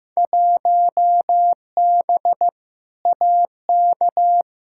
Morse code, heard as a keyed tone, about 15 wpm, 700 Hz. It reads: 1B AK
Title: Field Day Exchanges Encore - 15wpm